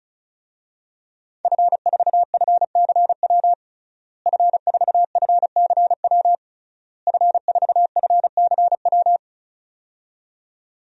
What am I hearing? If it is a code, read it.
Morse code, heard as a keyed tone, about 36 wpm, 700 Hz. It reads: F4FCW F4FCW F4FCW
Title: DXCC-100 Call Signs - 5 Characters - Repeated 3x - 35wpm